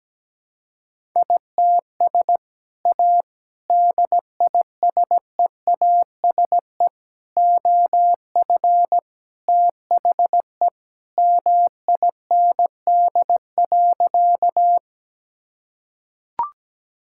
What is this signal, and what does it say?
Morse code, heard as a keyed tone, about 17 wpm, 700 Hz. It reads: ITS A DISEASE OF THE MIND.
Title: Sentences from Top 1000 Words - Review Part 6 - 17wpm